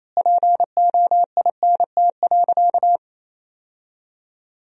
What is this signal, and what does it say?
Morse code, heard as a keyed tone, about 28 wpm, 700 Hz. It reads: POINT.
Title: 3x Word-Spacing - Sentences from Top 200 Words - 28wpm